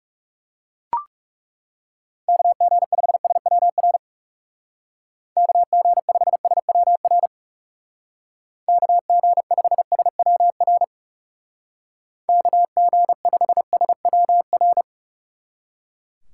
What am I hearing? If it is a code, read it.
Morse code, heard as a keyed tone, about 38 wpm, 700 Hz. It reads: KG5SWR KG5SWR KG5SWR KG5SWR